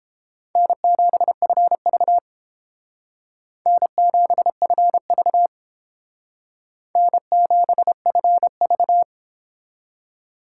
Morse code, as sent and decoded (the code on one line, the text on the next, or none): -. --... ..-. ...- / -. --... ..-. ...- / -. --... ..-. ...-
N7FV N7FV N7FV